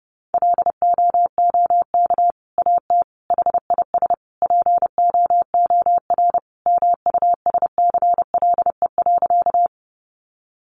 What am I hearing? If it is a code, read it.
Morse code, heard as a keyed tone, about 30 wpm, 700 Hz. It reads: LOOK AT HIS POOR MUSCLE.
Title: Sentences from Top 1300 Words 30wpm